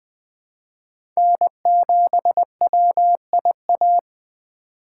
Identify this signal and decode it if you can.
Morse code, heard as a keyed tone, about 20 wpm, 700 Hz. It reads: N7WIA
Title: USA Call Signs - 3x Character Spacing - 20wpm